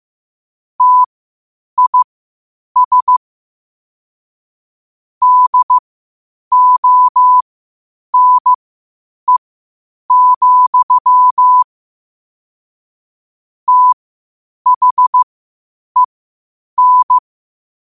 Morse code, heard as a keyed tone, about 15 wpm, spaced as for 8 wpm, 1000 Hz. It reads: TIS DONE, THEN
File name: qotdfiles / 8wpm